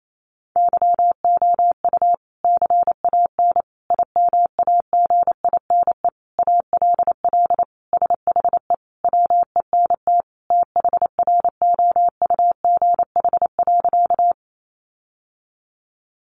Morse code, heard as a keyed tone, about 28 wpm, 700 Hz. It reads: YOU CAN IMAGINE ALL SHE WENT THROUGH.
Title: Sentences from Top 1400 Words 28wpm